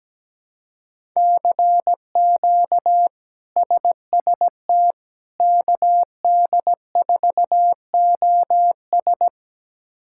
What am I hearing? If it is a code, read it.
Morse code, heard as a keyed tone, about 17 wpm, 700 Hz. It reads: CQ SST KD4OS
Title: K1USN SST QSO Practice - 17wpm